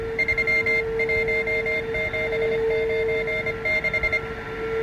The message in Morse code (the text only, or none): 31796